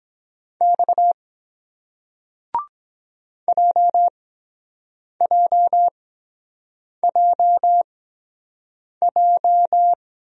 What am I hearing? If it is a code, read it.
Morse code, heard as a keyed tone, about 24 wpm, spaced as for 7 wpm, 700 Hz. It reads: X JJJJ